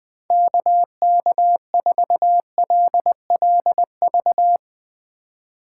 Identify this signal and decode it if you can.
Morse code, heard as a keyed tone, about 21 wpm, 700 Hz. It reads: KK4LLV
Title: USA Call Signs - 6 Characters - 2x Character Spacing - 20wpm